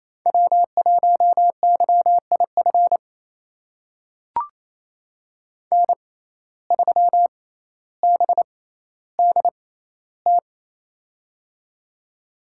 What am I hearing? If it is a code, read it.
Morse code, heard as a keyed tone, about 28 wpm, spaced as for 14 wpm, 700 Hz. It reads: W1YIF N3BDT
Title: USA Call Signs - 5 Characters - 6x Character Spacing - 28wpm